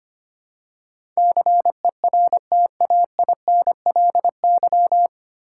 Morse code, heard as a keyed tone, about 25 wpm, 690 Hz. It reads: CERTAINLY
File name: practice / 25wpm / 9 Letter Words - First 3 Letters Given - 25wpm